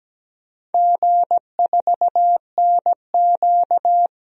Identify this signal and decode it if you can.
Morse code, heard as a keyed tone, about 17 wpm, 700 Hz. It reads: G4NQ
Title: DXCC-100 Call Signs - 4 Characters - 17wpm